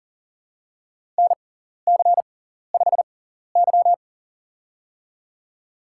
Morse code, heard as a keyed tone, about 42 wpm, spaced as for 14 wpm, 700 Hz. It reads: NC5Y
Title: USA Call Signs - 4 Characters - 6x Character Spacing - 40wpm